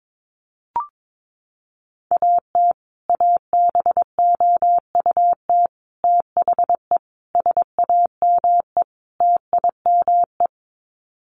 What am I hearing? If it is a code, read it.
Morse code, heard as a keyed tone, about 22 wpm, 700 Hz. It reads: AT ABOUT THE SAME TIME
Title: Sets of 5 Words Encore 22wpm